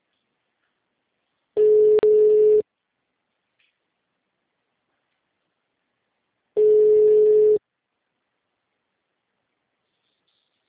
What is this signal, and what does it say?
Ringback tone (Europe, 425 Hz).
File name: out- -2010- - - .33777